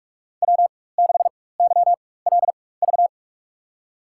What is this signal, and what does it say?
Morse code, heard as a keyed tone, about 47 wpm, spaced as for 21 wpm, 700 Hz. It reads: W6YLV
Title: USA Call Signs - 4x Character Spacing - 45wpm